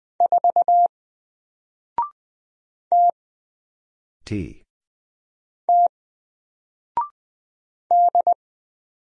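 Morse code, heard as a keyed tone, about 20 wpm, 700 Hz. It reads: 4 T T D
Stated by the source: Single letter and number Morse drill at 20 wpm, from N4LCD.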